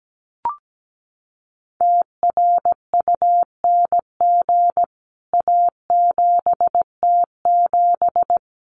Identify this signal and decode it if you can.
Morse code, heard as a keyed tone, about 17 wpm, 700 Hz. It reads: TRUNG A7T7
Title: CWT Exchange 17wpm